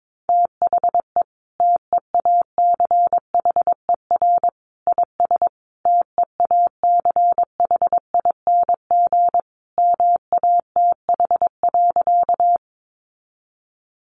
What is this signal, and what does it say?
Morse code, heard as a keyed tone, about 22 wpm, 700 Hz. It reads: THE TEACHER IS TEACHING MATH.
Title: Easy Phrases 22wpm